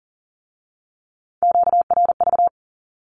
Morse code, heard as a keyed tone, about 38 wpm, 700 Hz. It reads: QRV
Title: Q-Codes with Spoken Meaning - 40wpm